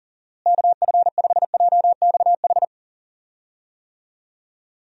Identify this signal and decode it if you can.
Morse code, heard as a keyed tone, about 40 wpm, 700 Hz. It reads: KF5JXH
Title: USA Call Signs - 6 Characters - Repeated 2x - 40wpm